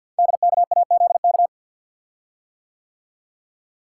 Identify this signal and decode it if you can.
Morse code, heard as a keyed tone, about 50 wpm, 700 Hz. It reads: DKAZK
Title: Code Groups - 5 Letters - 50wpm